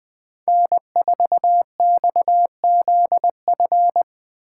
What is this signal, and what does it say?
Morse code, heard as a keyed tone, about 23 wpm, 700 Hz. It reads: N4XZF
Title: Speed Racing - USA 1x3 Call Signs 20wpm